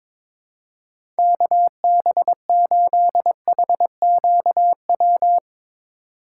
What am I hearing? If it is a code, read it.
Morse code, heard as a keyed tone, about 22 wpm, 700 Hz. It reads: KB8HQW
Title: USA Call Signs - 6 Characters - 4x Character Spacing - 22wpm